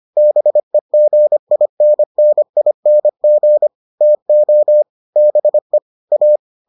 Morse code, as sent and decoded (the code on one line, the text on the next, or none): -... --. .. -. -. .. -. --. / - --- / -... / .-
BEGINNING TO BE A